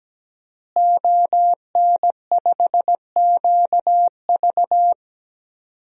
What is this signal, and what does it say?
Morse code, heard as a keyed tone, about 17 wpm, 700 Hz. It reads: ON5QV
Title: DXCC-100 Call Signs - 5 Characters - Repeated 2x - 17wpm